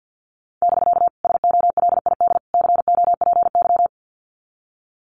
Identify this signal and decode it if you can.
Morse code, heard as a keyed tone, about 50 wpm, 700 Hz. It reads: <BK> SOLID COPY
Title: Common QSO Phrases Encore 50wpm